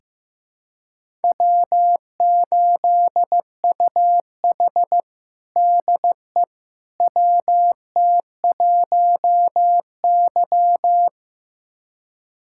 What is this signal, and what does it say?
Morse code, heard as a keyed tone, about 15 wpm, 700 Hz. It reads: W8UH DE WT1Y
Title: USA Callsign DE Callsign - 4 Characters - 15wpm